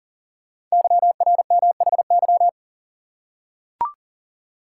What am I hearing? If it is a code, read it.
Morse code, heard as a keyed tone, about 40 wpm, 700 Hz. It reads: YRMHY
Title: Code Groups - 5 Letters - 40wpm